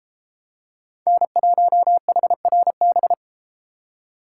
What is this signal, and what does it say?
Morse code, heard as a keyed tone, about 31 wpm, 700 Hz. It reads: N1HRB